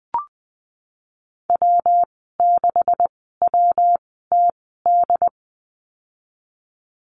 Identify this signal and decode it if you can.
Morse code, heard as a keyed tone, about 19 wpm, spaced as for 14 wpm, 700 Hz. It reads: W6WTD
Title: 140 Lesson 23g - Letter f - Call Signs 20wpm